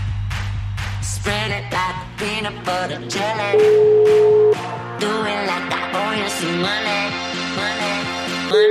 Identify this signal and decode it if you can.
Ringback tone (Europe, 425 Hz), with noise.